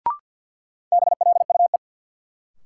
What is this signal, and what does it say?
Morse code, heard as a keyed tone, about 53 wpm, 700 Hz. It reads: BLUE
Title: Sentences from Top 1200 Words - New Word Drill Alt - 50wpm